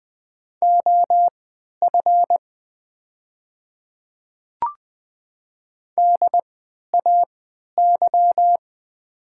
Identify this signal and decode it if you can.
Morse code, heard as a keyed tone, about 20 wpm, spaced as for 11 wpm, 700 Hz. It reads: OF DAY
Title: 3x Character Spacing - Top 100 Words - 20wpm